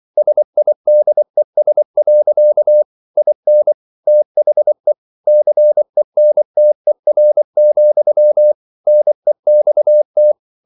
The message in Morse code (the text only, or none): SIDES. IN THE CENTER, NEXT